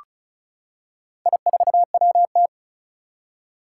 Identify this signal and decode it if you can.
Morse code, heard as a keyed tone, about 36 wpm, 700 Hz. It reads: I4WT
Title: Four Challenging Letter-Number Combinations 35wpm